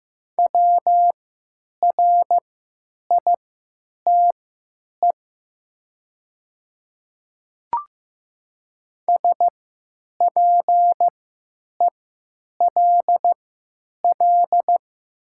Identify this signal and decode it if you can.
Morse code, heard as a keyed tone, about 15 wpm, spaced as for 9 wpm, 700 Hz. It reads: WRITE SPELL
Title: 3x Character Spacing - Top 200 Words - 15wpm